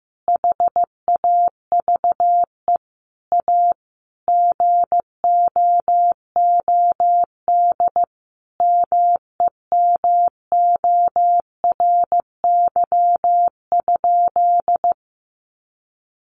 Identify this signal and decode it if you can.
Morse code, heard as a keyed tone, about 15 wpm, 700 Hz. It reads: HAVE A GOOD MEMORY?
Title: Sentences from Top 1300 Words 15wpm